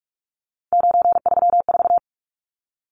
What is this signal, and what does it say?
Morse code, heard as a keyed tone, about 44 wpm, 700 Hz. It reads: 934
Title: Numbers - 3-Digit 45wpm